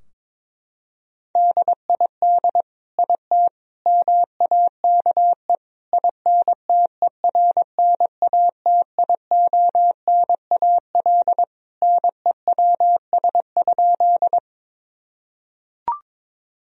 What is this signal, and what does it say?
Morse code, heard as a keyed tone, about 22 wpm, 700 Hz. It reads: DID IT MAKE INTERNATIONAL NEWS?
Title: Sentences from Top 1000 Words - Review Part 4 - 22wpm